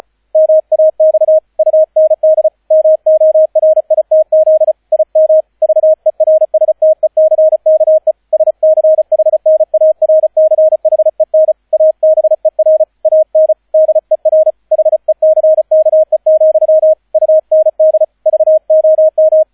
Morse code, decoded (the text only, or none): MAX UND MORITZ IM VERSTECKE SCHNARCHEN ABER AN DER HECKE, UND VOM